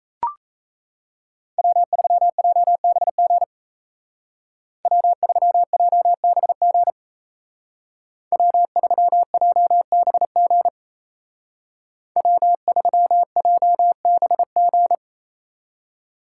Morse code, heard as a keyed tone, about 37 wpm, 700 Hz. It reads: W3JBG W3JBG W3JBG W3JBG